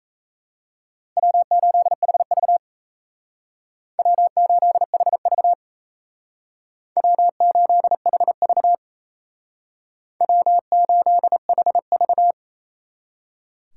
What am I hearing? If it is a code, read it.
Morse code, heard as a keyed tone, about 36 wpm, 700 Hz. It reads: W8HV W8HV W8HV W8HV